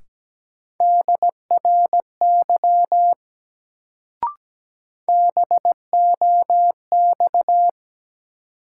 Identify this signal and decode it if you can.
Morse code, heard as a keyed tone, about 17 wpm, 700 Hz. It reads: DRY BOX
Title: Sentences from Top 500 Words - New Word Drill - 17wpm